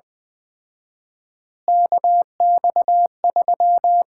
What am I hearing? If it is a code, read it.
Morse code, heard as a keyed tone, about 21 wpm, 700 Hz. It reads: KX3
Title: Common QSO Phrases - 4x Word-Spacing - 20wpm